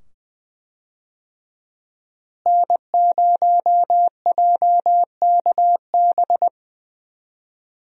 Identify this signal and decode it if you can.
Morse code, heard as a keyed tone, about 20 wpm, 700 Hz. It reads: N0JKB
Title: USA Call Signs - 5 Characters - Repeated 2x - 20wpm